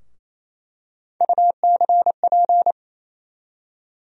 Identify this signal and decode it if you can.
Morse code, heard as a keyed tone, about 28 wpm, 700 Hz. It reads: UCP
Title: USA Call Sign Postfixes - 3 Letters 28wpm